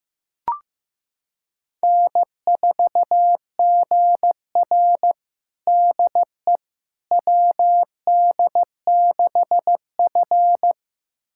Morse code, heard as a keyed tone, about 15 wpm, 700 Hz. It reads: N4GR DE WD6F
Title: USA Callsign DE Callsign - 4 Characters - 15wpm